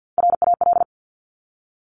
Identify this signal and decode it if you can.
Morse code, heard as a keyed tone, about 50 wpm, 700 Hz. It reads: RAL
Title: Top Three Letter Combinations in Words 50wpm